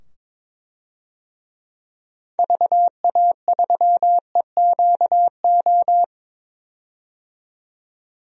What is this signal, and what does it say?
Morse code, heard as a keyed tone, about 22 wpm, 700 Hz. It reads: VA3EQO
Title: DXCC-100 Call Signs - 6 Characters - 22wpm